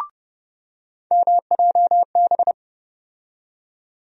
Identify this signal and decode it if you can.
Morse code, heard as a keyed tone, about 30 wpm, 700 Hz. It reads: MJB
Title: USA Call Sign Postfixes - 3 Letters 30wpm